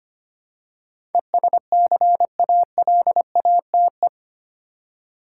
Morse code, heard as a keyed tone, about 25 wpm, 700 Hz. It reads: ESCALATE
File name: 8 Letter Words - First 3 Letters Given - 25wpm